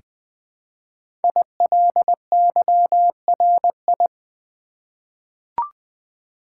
Morse code, heard as a keyed tone, about 20 wpm, 700 Hz. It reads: ILYRI
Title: Code Groups - 5 Letters - 20wpm